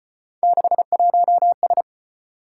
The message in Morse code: 61S

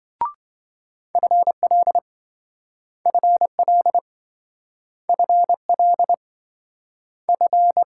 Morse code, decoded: FL FL FL F

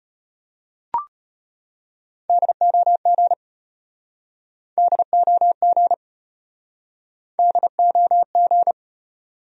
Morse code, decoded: DOG DOG DOG

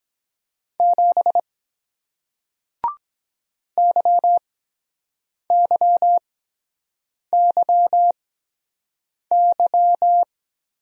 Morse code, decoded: 7 YYYY

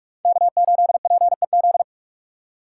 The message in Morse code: K8PEZ

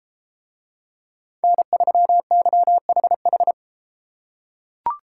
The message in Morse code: N3YHH

